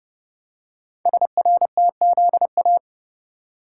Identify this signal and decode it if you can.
Morse code, heard as a keyed tone, about 30 wpm, 700 Hz. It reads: SRTZA
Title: Code Groups - 5 Letters - 30wpm